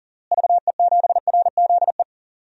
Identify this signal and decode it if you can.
Morse code, heard as a keyed tone, about 37 wpm, 700 Hz. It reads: VE7RZE
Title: DXCC-100 Call Signs - 6 Characters - Repeated 3x - 40wpm